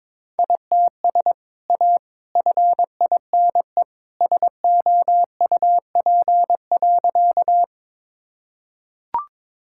Morse code, heard as a keyed tone, about 22 wpm, 700 Hz. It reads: ITS A FINE SOUP.
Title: Sentences from Top 1000 Words - Review Part 6 - 22wpm